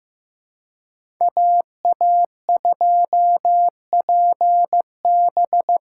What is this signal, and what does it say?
Morse code, heard as a keyed tone, about 18 wpm, 700 Hz. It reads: AA2PB
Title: Speed Racing - USA 2x2 Call Signs 15wpm